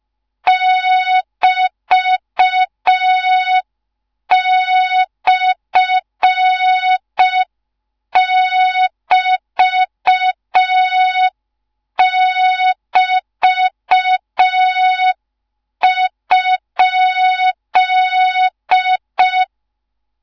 Morse code, heard as a keyed tone, about 5 wpm, 3700 Hz. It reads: <BT>/<BT><BT>?
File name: Morze-HA2MN / hang / 13IRASJELEK1 HA2MN